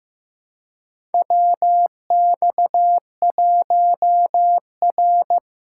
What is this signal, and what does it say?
Morse code, heard as a keyed tone, about 15 wpm, 700 Hz. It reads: WX1R
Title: USA Callsign DE Callsign - 4 Characters - 15wpm